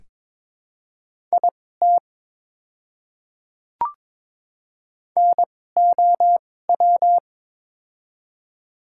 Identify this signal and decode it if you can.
Morse code, heard as a keyed tone, about 22 wpm, spaced as for 16 wpm, 700 Hz. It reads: IT NOW
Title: 2x Character Spacing - Top 200 Words - 22wpm